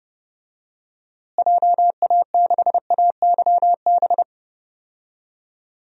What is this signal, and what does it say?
Morse code, heard as a keyed tone, about 31 wpm, 700 Hz. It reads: JA6AYB
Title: DXCC-100 Call Signs - 6 Characters - Repeated 2x - 30wpm